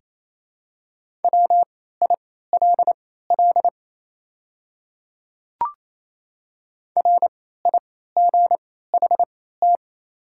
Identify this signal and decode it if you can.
Morse code, heard as a keyed tone, about 27 wpm, spaced as for 16 wpm, 690 Hz. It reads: WILL RIGHT
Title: 3x Character Spacing - Top 200 Words - 28wpm